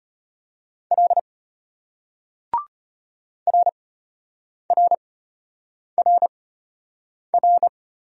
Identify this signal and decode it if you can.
Morse code, heard as a keyed tone, about 34 wpm, spaced as for 8 wpm, 700 Hz. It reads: L RRRR